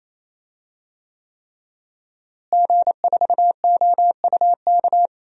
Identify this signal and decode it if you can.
Morse code, heard as a keyed tone, about 28 wpm, 700 Hz. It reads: G4OUK
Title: DXCC-100 Call Signs - 5 Characters - Repeated 3x - 28wpm